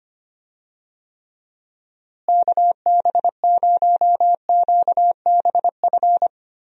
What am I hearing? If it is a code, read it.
Morse code, heard as a keyed tone, about 25 wpm, 700 Hz. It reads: KB0QBF
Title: USA Call Signs - 6 Characters - Repeated 2x - 25wpm